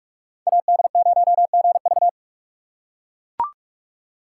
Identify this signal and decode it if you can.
Morse code, heard as a keyed tone, about 42 wpm, 700 Hz. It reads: AD0GV